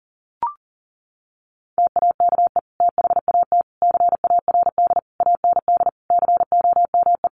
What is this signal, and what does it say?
Morse code, heard as a keyed tone, about 40 wpm, 700 Hz. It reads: TAKE THAT CARD AND COME
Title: Sentences from Top 1400 Words 40wpm